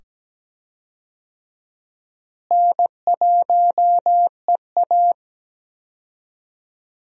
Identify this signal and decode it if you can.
Morse code, heard as a keyed tone, about 17 wpm, 700 Hz. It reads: N1EA
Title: USA Call Signs - 4 Characters - Repeated 2x - 17wpm